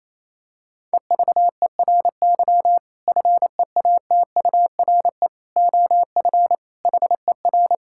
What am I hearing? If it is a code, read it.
Morse code, heard as a keyed tone, about 28 wpm, 700 Hz. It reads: EVERY FEATURE OF HER
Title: Sentences from Top 1000 Words - Review Part 5 - 28wpm